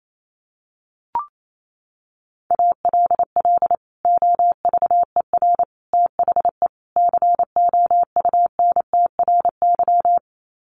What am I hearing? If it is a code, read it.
Morse code, heard as a keyed tone, about 28 wpm, 700 Hz. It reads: ALL OVER THE COUNTRY
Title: Sets of 4 Words 28wpm